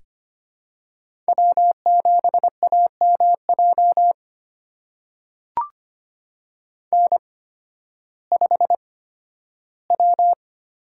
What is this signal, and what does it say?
Morse code, heard as a keyed tone, about 25 wpm, spaced as for 13 wpm, 700 Hz. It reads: W7AMJ N5W
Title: USA Call Signs - 5 Characters - 8x Character Spacing - 25wpm